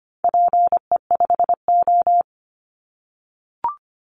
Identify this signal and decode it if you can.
Morse code, heard as a keyed tone, about 25 wpm, 700 Hz. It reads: PE5O
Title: Call Signs - 25wpm